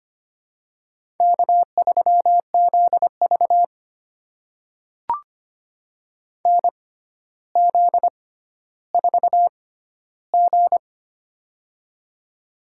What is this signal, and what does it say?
Morse code, heard as a keyed tone, about 25 wpm, spaced as for 13 wpm, 700 Hz. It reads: K3ZV NZ4G
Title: USA Call Signs - 4 Characters - 6x Character Spacing - 25wpm